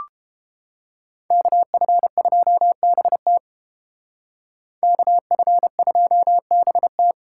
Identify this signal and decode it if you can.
Morse code, heard as a keyed tone, about 29 wpm, 700 Hz. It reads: KF2BT KF2BT